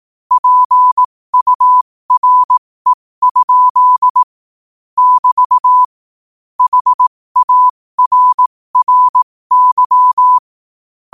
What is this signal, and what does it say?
Morse code, heard as a keyed tone, about 18 wpm, spaced as for 15 wpm, 1000 Hz. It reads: PURE? <BT> HARRY
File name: qotdfiles / 15wpm